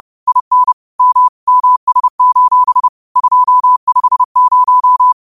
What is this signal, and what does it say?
Morse code, heard as a keyed tone, about 30 wpm, 1000 Hz. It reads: IN M MS: 250
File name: qotdfiles / 30wpm